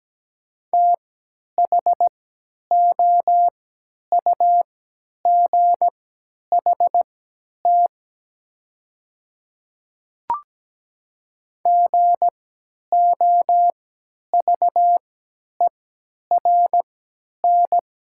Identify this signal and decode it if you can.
Morse code, heard as a keyed tone, about 17 wpm, spaced as for 10 wpm, 700 Hz. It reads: THOUGHT GOVERN
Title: 3x Character Spacing - Top 500 Words - 17wpm